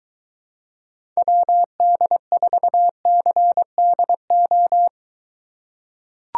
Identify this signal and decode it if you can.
Morse code, heard as a keyed tone, about 23 wpm, 700 Hz. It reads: WD4CDO